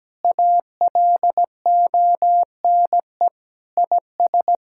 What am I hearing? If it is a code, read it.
Morse code, heard as a keyed tone, about 17 wpm, 700 Hz. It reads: ALONE IS
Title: Sentences from Top 1600 Words 17wpm